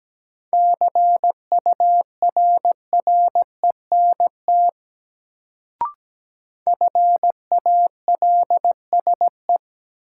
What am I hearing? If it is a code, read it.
Morse code, heard as a keyed tone, about 17 wpm, 700 Hz. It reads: CURRENT FALSE
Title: Sentences from Top 1600 Words - New Word Drill Alt - 17wpm